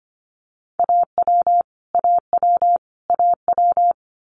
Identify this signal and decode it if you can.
Morse code, heard as a keyed tone, about 25 wpm, 700 Hz. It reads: AW AW AW
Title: Multi-letter Phonemes Repeated 3x 25wpm